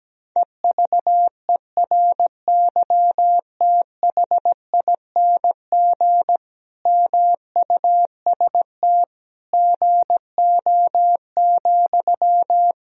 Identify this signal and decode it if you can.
Morse code, heard as a keyed tone, about 17 wpm, 700 Hz. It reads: EVERYTHING MUST GO,
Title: Sentences from Top 1000 Words - Review Part 4 - 17wpm